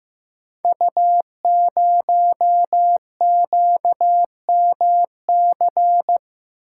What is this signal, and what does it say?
Morse code, heard as a keyed tone, about 15 wpm, 700 Hz. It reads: U0QMC
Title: Code Groups - 5 Letter:Numbers - 15wpm